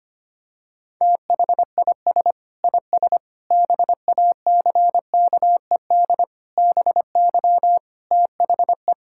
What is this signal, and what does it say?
Morse code, heard as a keyed tone, about 25 wpm, 700 Hz. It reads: THIS IS BACKED BY THE
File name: practice / 25wpm / Sentences with 9-Word Phrases - 25wpm